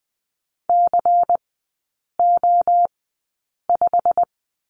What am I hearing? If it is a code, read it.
Morse code, heard as a keyed tone, about 20 wpm, spaced as for 8 wpm, 700 Hz. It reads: CO5
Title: 093 Lesson 16e - Letter c - Three Characters 20wpm